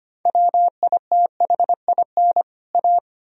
Morse code, heard as a keyed tone, about 25 wpm, 700 Hz. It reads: WITHIN A
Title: Sentences from Top 1000 Words - Review Part 4 - 25wpm